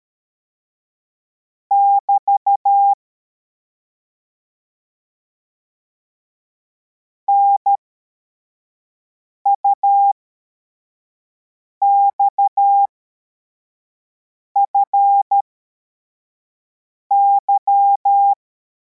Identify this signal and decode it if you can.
Morse code, heard as a keyed tone, about 13 wpm, spaced as for 4 wpm, 800 Hz. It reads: <BT> NUXFY
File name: formation / cw / Text74